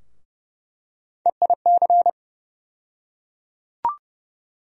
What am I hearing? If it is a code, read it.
Morse code, heard as a keyed tone, about 31 wpm, 700 Hz. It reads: EIC
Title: USA Call Sign Postfixes - 3 Letters 30wpm